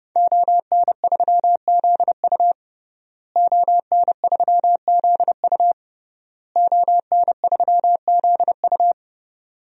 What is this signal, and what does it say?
Morse code, heard as a keyed tone, about 32 wpm, 700 Hz. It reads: ON3ZU ON3ZU ON3ZU
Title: DXCC-100 Call Signs - 5 Characters - Repeated 3x - 30wpm